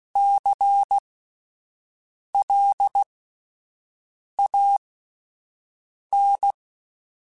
Morse code, heard as a keyed tone, about 16 wpm, spaced as for 5 wpm, 800 Hz. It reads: CLAN
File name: Alphabetical / Alphabetical mp3s / Cs @5